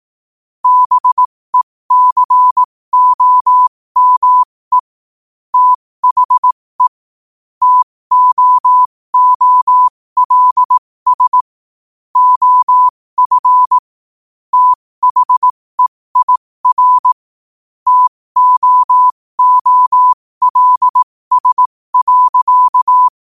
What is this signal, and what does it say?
Morse code, heard as a keyed tone, about 18 wpm, spaced as for 15 wpm, 1000 Hz. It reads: BECOME THE TOOLS OF THEIR TOOLS.
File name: qotdfiles / 15wpm